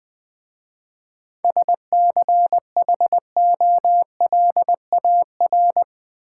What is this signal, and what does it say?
Morse code, heard as a keyed tone, about 20 wpm, 700 Hz. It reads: SCHOLAR
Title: 7 Letter Words - First 3 Letters Given - 20wpm